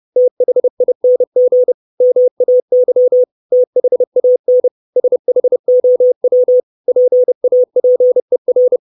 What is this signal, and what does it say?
Morse code, heard as a keyed tone, about 30 wpm, 500 Hz. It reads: THING MAY THAN SHOW PAPER
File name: 632 Words 30WPM